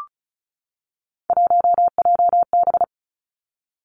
Morse code, heard as a keyed tone, about 36 wpm, 700 Hz. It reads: 1JB
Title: Three Challenging Letter-Number Combinations 35wpm